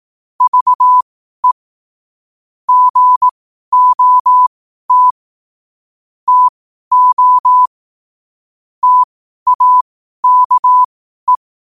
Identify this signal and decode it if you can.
Morse code, heard as a keyed tone, about 18 wpm, spaced as for 12 wpm, 1000 Hz. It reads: VE GOT TO TAKE
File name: qotdfiles / 12wpm